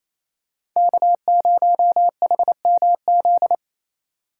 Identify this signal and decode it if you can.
Morse code, heard as a keyed tone, about 28 wpm, 700 Hz. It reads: K0HMZ
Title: K1USN SST QSO Practice - 28wpm